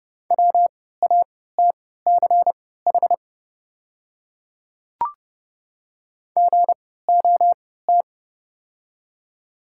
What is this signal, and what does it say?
Morse code, heard as a keyed tone, about 30 wpm, spaced as for 17 wpm, 700 Hz. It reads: WATCH GOT
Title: 3x Character Spacing - Top 300 Words - 30wpm